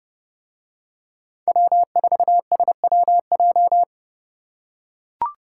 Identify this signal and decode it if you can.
Morse code, heard as a keyed tone, about 30 wpm, 700 Hz. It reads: W4SWJ